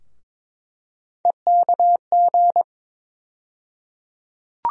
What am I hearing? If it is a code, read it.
Morse code, heard as a keyed tone, about 21 wpm, 700 Hz. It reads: EKG
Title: USA Call Sign Postfixes - 3 Letters 22wpm